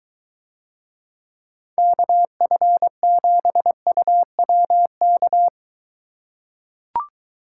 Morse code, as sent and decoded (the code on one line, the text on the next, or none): -.- ..-. --... ..- .-- -.-
KF7UWK